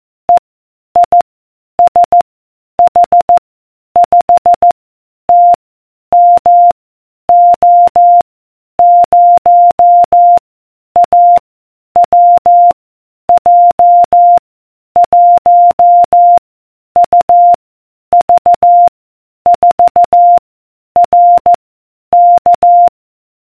Morse code, heard as a keyed tone, about 14 wpm, 700 Hz. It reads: E I S H 5 T M O 0 A W J 1 U V 4 R K